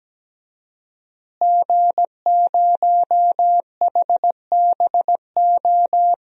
Morse code, heard as a keyed tone, about 17 wpm, 700 Hz. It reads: G0HBO
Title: DXCC-100 Call Signs - 5 Characters - Repeated 3x - 17wpm